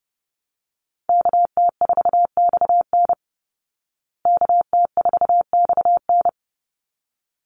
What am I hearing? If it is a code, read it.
Morse code, heard as a keyed tone, about 30 wpm, 700 Hz. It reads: KT4XN KT4XN
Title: Call Signs Repeated 3x - 30wpm